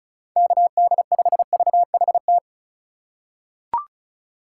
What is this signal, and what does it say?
Morse code, heard as a keyed tone, about 34 wpm, 700 Hz. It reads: KD5VHT
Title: USA Call Signs - 6 Characters - 2x Character Spacing - 35wpm